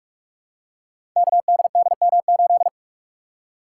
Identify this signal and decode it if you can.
Morse code, heard as a keyed tone, about 44 wpm, 700 Hz. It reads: KDDM8
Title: Code Groups - 5 Letter:Numbers - 45wpm